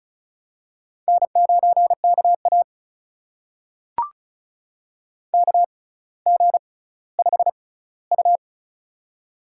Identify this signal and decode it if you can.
Morse code, heard as a keyed tone, about 36 wpm, spaced as for 18 wpm, 700 Hz. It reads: N9KA KG5U
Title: USA Call Signs - 4 Characters - 6x Character Spacing - 35wpm